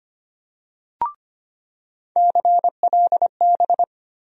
Morse code, heard as a keyed tone, about 25 wpm, 700 Hz. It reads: CLB
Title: USA Call Sign Postfixes - 3 Letters 25wpm